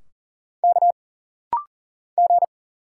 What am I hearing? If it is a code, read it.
Morse code, heard as a keyed tone, about 40 wpm, 700 Hz. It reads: K G
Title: Single Letters - Rapid-Fire 40wpm